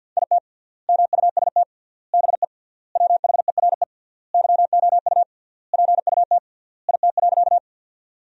Morse code, heard as a keyed tone, about 50 wpm, 700 Hz. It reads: IT MUST BE WHERE YOU PUT IT.
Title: 3x Word-Spacing - Sentences from Top 200 Words - 50wpm